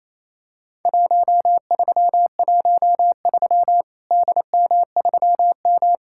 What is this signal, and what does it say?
Morse code, heard as a keyed tone, about 28 wpm, 700 Hz. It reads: 1313 DM3M
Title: Worked All Europe DX Contest - QTC Exchange - 28wpm